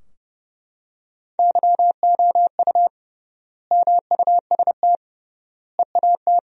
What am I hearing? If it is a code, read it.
Morse code, heard as a keyed tone, about 30 wpm, 700 Hz. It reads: YOU MUST EAT
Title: 3x Word-Spacing - Sentences from Top 300 Words - 30wpm